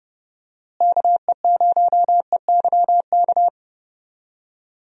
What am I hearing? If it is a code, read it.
Morse code, heard as a keyed tone, about 30 wpm, 700 Hz. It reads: KE0EYK
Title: USA Call Signs - 6 Characters - Repeated 3x - 30wpm